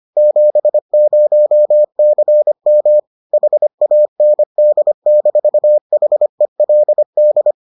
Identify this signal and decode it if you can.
Morse code, heard as a keyed tone, about 25 wpm, 600 Hz. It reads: 70CM HAND-HELD